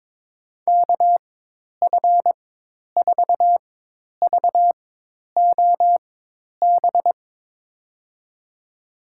Morse code, heard as a keyed tone, about 22 wpm, spaced as for 10 wpm, 700 Hz. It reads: KF4VOB
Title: USA Call Signs - 4x Character Spacing - 22wpm